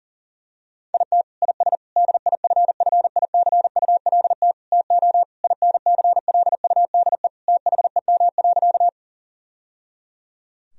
Morse code, heard as a keyed tone, about 40 wpm, 700 Hz. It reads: IT IS DIFFICULT TO INCLUDE THEM.
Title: Sentences from Top 1000 Words - Review Part 3 - 40wpm